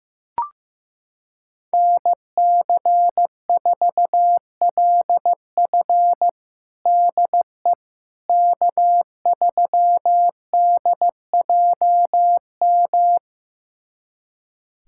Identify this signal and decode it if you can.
Morse code, heard as a keyed tone, about 15 wpm, 700 Hz. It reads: NC4LF DE K3DJM
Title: USA Callsign DE Callsign - 5 Characters - 15wpm